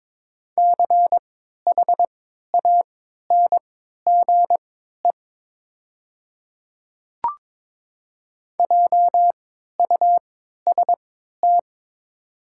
Morse code, heard as a keyed tone, about 22 wpm, spaced as for 12 wpm, 690 Hz. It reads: CHANGE JUST
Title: 3x Character Spacing - Top 200 Words - 22wpm